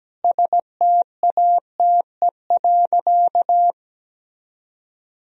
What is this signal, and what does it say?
Morse code, heard as a keyed tone, about 17 wpm, 700 Hz. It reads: STATE.
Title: Sentences from Top 1000 Words - Review Part 2 - 17wpm